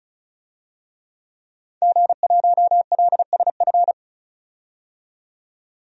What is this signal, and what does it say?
Morse code, heard as a keyed tone, about 34 wpm, 700 Hz. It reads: G1LSF
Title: DXCC-100 Call Signs - 5 Characters - 35wpm